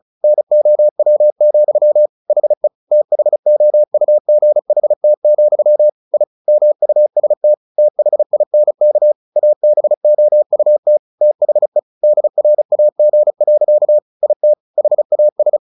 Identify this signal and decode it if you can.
Morse code, heard as a keyed tone, about 34 wpm, 600 Hz. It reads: NOW, HE THOUGHT, I MUST THINK ABOUT THE DRAG. IT HAS
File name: the old man and the sea 2 35wpm